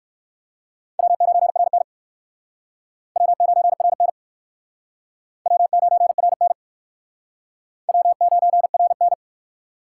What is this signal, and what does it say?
Morse code, heard as a keyed tone, about 56 wpm, 700 Hz. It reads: W9RN W9RN W9RN W9RN
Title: Speed Racing - USA Call Signs 45wpm